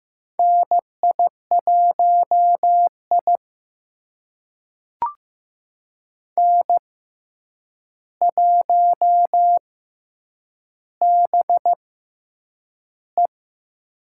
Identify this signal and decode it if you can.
Morse code, heard as a keyed tone, about 15 wpm, spaced as for 8 wpm, 700 Hz. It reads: NI1I N1BE
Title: USA Call Signs - 4 Characters - 6x Character Spacing - 15wpm